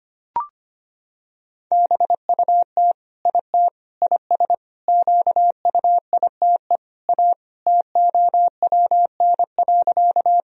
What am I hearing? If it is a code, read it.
Morse code, heard as a keyed tone, about 25 wpm, 700 Hz. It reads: BUT IT IS QUITE A TOWN.
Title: Sentences from Top 1000 Words - Review Part 2 - 25wpm